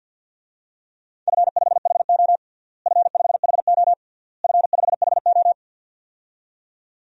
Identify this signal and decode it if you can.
Morse code, heard as a keyed tone, about 51 wpm, 700 Hz. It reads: F5HO F5HO F5HO
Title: DXCC-100 Call Signs - 4 Characters - Repeated 3x - 50wpm